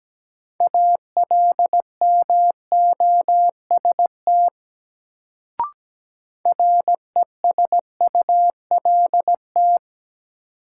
Morse code, heard as a keyed tone, about 17 wpm, 700 Hz. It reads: ALMOST RESULT
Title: Sentences from Top 700 Words - New Word Drill - 17wpm